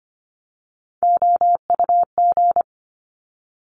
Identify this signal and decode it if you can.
Morse code, heard as a keyed tone, about 25 wpm, 700 Hz. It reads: OUG
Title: Top Three Letter Combinations in Words 25wpm